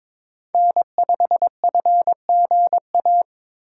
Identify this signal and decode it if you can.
Morse code, heard as a keyed tone, about 22 wpm, 700 Hz. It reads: N5FGA
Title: USA Call Signs - 5 Characters - 4x Character Spacing - 22wpm